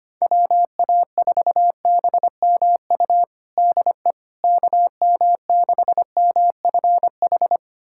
Morse code, heard as a keyed tone, about 25 wpm, 700 Hz. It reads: WA4BMU DE KM6MFH
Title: USA Callsign DE Callsign 25wpm